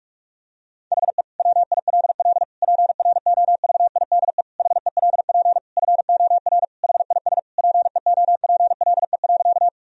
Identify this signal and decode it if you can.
Morse code, heard as a keyed tone, about 46 wpm, 700 Hz. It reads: HE WILL PROVIDE HELP FOR HIS PEOPLE.
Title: Sentences from Top 1000 Words - Review Part 3 - 45wpm